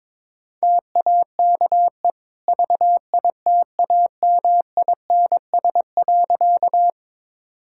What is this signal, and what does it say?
Morse code, heard as a keyed tone, about 22 wpm, 690 Hz. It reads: TAKE VITAMINS.
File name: practice / 22wpm / Easy Sentences with Hints - 22wpm